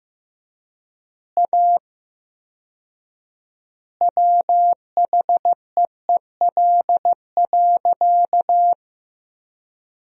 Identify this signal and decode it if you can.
Morse code, heard as a keyed tone, about 15 wpm, 700 Hz. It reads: A WHEEL.
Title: 4x Word-Spacing - Sentences from Top 500 Words - 15wpm